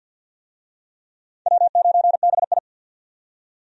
Morse code, heard as a keyed tone, about 49 wpm, 700 Hz. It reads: W9BI
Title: USA Call Signs - 4 Characters - 50wpm